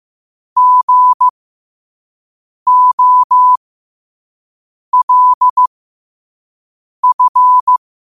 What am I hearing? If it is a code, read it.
Morse code, heard as a keyed tone, about 15 wpm, spaced as for 5 wpm, 1000 Hz. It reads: GOLF